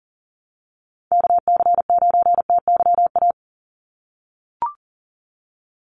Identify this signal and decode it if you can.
Morse code, heard as a keyed tone, about 40 wpm, 700 Hz. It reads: KC9TYA
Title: NA SOTA Chaser Callsigns 40wpm